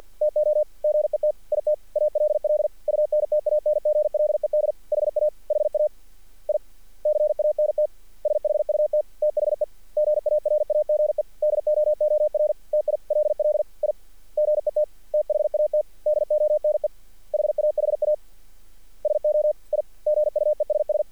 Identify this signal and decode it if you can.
Morse code, heard as a keyed tone, about 48 wpm, 610 Hz. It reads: TO GET IT ALL UNTANGLED HA HA I CANT SHUT THE GARAGE DOOR TILL I GET THAT DONE HAHA SO I GUESS
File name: archived / CRSnet